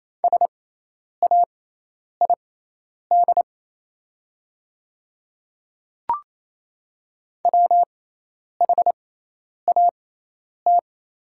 Morse code, heard as a keyed tone, about 28 wpm, spaced as for 10 wpm, 700 Hz. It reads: SAID WHAT
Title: 6x Character Spacing - Top 100 Words - 28wpm